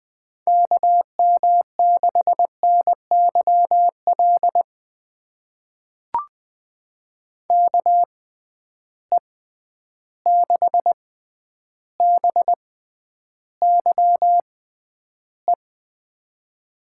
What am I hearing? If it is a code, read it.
Morse code, heard as a keyed tone, about 20 wpm, spaced as for 10 wpm, 700 Hz. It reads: KM6NYL KE6BYE
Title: USA Call Signs - 6 Characters - 6x Character Spacing - 20wpm